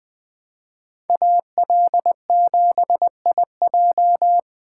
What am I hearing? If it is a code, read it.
Morse code, heard as a keyed tone, about 20 wpm, 700 Hz. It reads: AL7IJ
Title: Speed Racing - USA Call Signs 17wpm